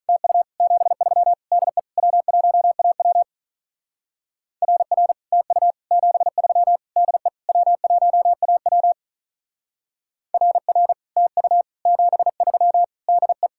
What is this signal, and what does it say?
Morse code, heard as a keyed tone, about 42 wpm, 700 Hz. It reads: TU 73 DE W1AW RR TU 73 DE W1AW RR TU 73 DE
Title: Speed-Racing Common QSO Phrases Encore - 35wpm